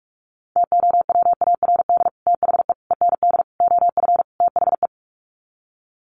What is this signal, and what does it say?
Morse code, heard as a keyed tone, about 44 wpm, 700 Hz. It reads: TOWARD THE END OF THE
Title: Sets of 5 Words 45wpm